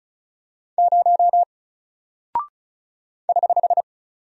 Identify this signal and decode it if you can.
Morse code, heard as a keyed tone, about 36 wpm, 700 Hz. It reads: 0 <HH>
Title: Single Character (Letter:Number:Punctuation:Prosign) Alt 35wpm